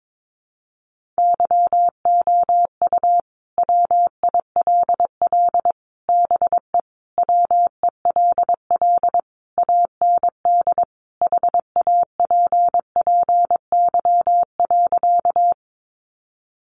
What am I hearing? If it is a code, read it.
Morse code, heard as a keyed tone, about 22 wpm, 700 Hz. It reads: YOU WILL BE WELL AND HAPPY.
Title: Sentences from Top 1000 Words 22wpm